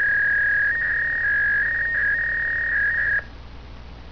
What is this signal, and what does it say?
Fax machine handshake.